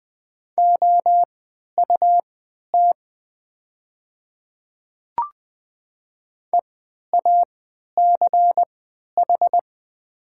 Morse code, heard as a keyed tone, about 20 wpm, spaced as for 11 wpm, 700 Hz. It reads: OUT EACH
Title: 3x Character Spacing - Top 300 Words - 20wpm